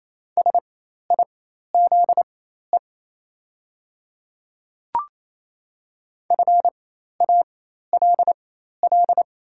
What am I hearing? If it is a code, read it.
Morse code, heard as a keyed tone, about 27 wpm, spaced as for 13 wpm, 690 Hz. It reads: SIZE FALL
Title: 4x Character Spacing - Top 500 Words - 28wpm